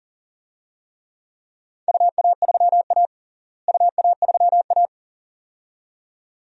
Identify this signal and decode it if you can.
Morse code, heard as a keyed tone, about 40 wpm, 700 Hz. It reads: UA3A UA3A
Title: DXCC-100 Call Signs - 4 Characters - Repeated 2x - 40wpm